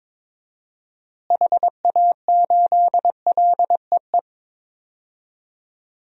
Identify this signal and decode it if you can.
Morse code, heard as a keyed tone, about 22 wpm, 700 Hz. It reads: HA8LEE
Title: DXCC-100 Call Signs - 6 Characters - Repeated 2x - 22wpm